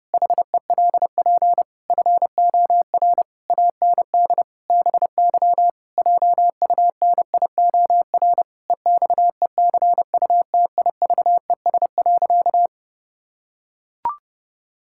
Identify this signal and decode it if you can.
Morse code, heard as a keyed tone, about 32 wpm, 700 Hz. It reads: HELP FOR AND BY JUNIOR EXECUTIVES.
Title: Pangram Challenge v2 - 30wpm